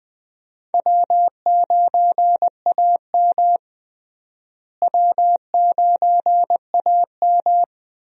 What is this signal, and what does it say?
Morse code, heard as a keyed tone, about 20 wpm, 700 Hz. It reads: W9AM W9AM
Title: USA Call Signs - 4 Characters - Repeated 3x - 20wpm